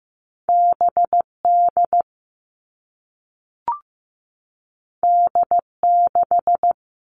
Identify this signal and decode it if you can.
Morse code, heard as a keyed tone, about 15 wpm, 700 Hz. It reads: BD D6
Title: All Cut Numbers - 2-Digit 15wpm